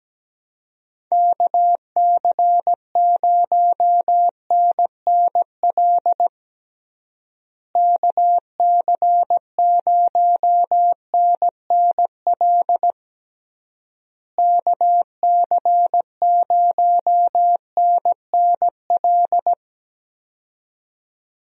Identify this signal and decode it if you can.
Morse code, heard as a keyed tone, about 17 wpm, 700 Hz. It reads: KC0NNL KC0NNL KC0NNL
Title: USA Call Signs - 6 Characters - Repeated 3x - 17wpm